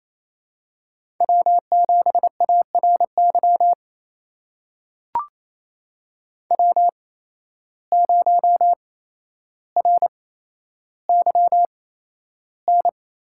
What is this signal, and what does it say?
Morse code, heard as a keyed tone, about 27 wpm, spaced as for 12 wpm, 700 Hz. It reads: W7ARY W0RYN
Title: USA Call Signs - 5 Characters - 8x Character Spacing - 28wpm